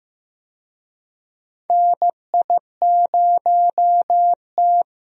Morse code, heard as a keyed tone, about 15 wpm, 700 Hz. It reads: NI0T
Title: USA Call Signs - 4 Characters - 15wpm